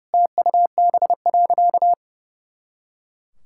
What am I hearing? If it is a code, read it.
Morse code, heard as a keyed tone, about 30 wpm, 700 Hz. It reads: TUB.
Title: Pangram Challenge v2 - 30wpm